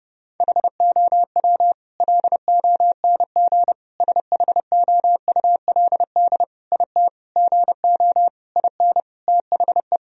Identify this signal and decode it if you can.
Morse code, heard as a keyed tone, about 30 wpm, 700 Hz. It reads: HOW LONG SHOULD IT GO IN THE
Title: Sentences from Top 1000 Words - Review Part 4 - 30wpm